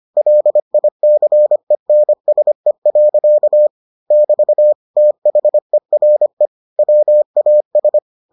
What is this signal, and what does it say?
Morse code, heard as a keyed tone, about 25 wpm, 600 Hz. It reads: LICENSE. <BT> THERE WAS